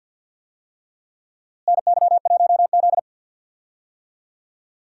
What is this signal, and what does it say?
Morse code, heard as a keyed tone, about 46 wpm, 700 Hz. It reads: NY1Z
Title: USA Call Signs - 4 Characters - Repeated 3x - 50wpm